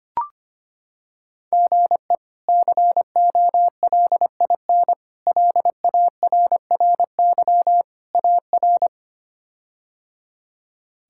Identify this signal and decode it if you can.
Morse code, heard as a keyed tone, about 25 wpm, 700 Hz. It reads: GE COLIN LARRY AR
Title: K1USN SST QSO Practice - 25wpm